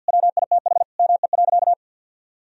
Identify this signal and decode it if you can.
Morse code, heard as a keyed tone, about 52 wpm, 700 Hz. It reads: WITH ME.
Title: Sentences from Top 1000 Words - Review Part 1 - 50wpm